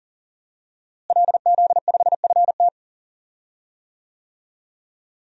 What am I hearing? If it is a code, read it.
Morse code, heard as a keyed tone, about 40 wpm, 700 Hz. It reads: LZ5FT
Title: DXCC-100 Call Signs - 5 Characters - Repeated 2x - 40wpm